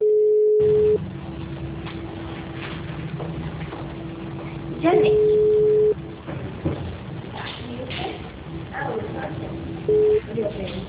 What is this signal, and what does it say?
Ringback tone (Europe, 425 Hz), with speech.
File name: out- -2007- - - .83740